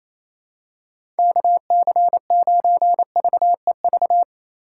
Morse code, heard as a keyed tone, about 28 wpm, 700 Hz. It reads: KC9VEV
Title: USA Call Signs - 6 Characters - 4x Character Spacing - 28wpm